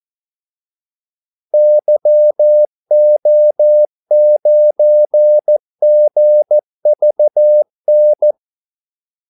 Morse code, heard as a keyed tone, about 14 wpm, 600 Hz. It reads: YO9GVN
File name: Callsigns-14wpm